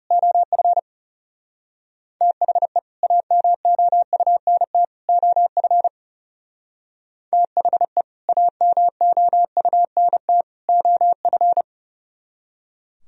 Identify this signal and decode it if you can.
Morse code, heard as a keyed tone, about 36 wpm, 700 Hz. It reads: OF THE AMOUNT OF THE AMOUNT OF
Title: Speed-Racing - Sets of 3 Words 30wpm